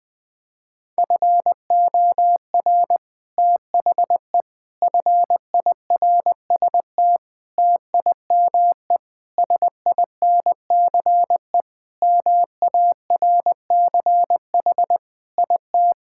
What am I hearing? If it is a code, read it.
Morse code, heard as a keyed tone, about 20 wpm, 700 Hz. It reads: FOR THE FIRST TIME SINCE MARCH IT
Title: Sentences with 5-Word Phrases - Spoken Beginning - 20wpm